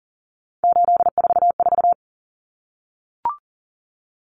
Morse code, heard as a keyed tone, about 40 wpm, 700 Hz. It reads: 844
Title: Numbers - 3-Digit 40wpm